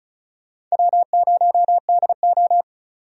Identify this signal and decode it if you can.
Morse code, heard as a keyed tone, about 36 wpm, 700 Hz. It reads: W0DO
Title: USA Call Signs - 4 Characters - 35wpm